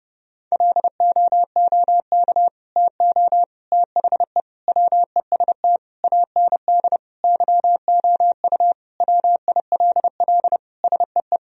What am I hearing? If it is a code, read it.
Morse code, heard as a keyed tone, about 30 wpm, 700 Hz. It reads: LOOK TO THE WEST AND YOU WILL SEE
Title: Sentences from Top 1000 Words - Review Part 2 - 30wpm